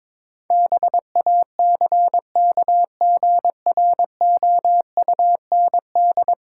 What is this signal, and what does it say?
Morse code, heard as a keyed tone, about 22 wpm, 700 Hz. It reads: BACKGROUND
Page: morse-code-ninja-practice-22wpm